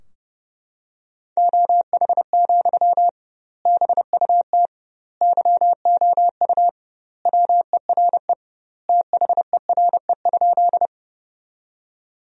Morse code, heard as a keyed tone, about 32 wpm, 700 Hz. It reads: OH, BUT YOU WERE THERE?
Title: 2x Word-Spacing - Sentences from Top 500 Words - 30wpm